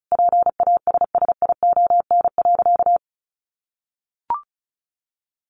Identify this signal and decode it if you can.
Morse code, heard as a keyed tone, about 34 wpm, 700 Hz. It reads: PASSION.
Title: Sentences from Top 1300 Words 35wpm